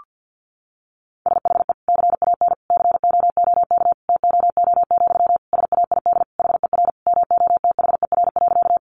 Morse code, heard as a keyed tone, about 50 wpm, 700 Hz. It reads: SHE CAN COOK TOO, SAID HER MOTHER.
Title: Sentences from Top 700 Words 50wpm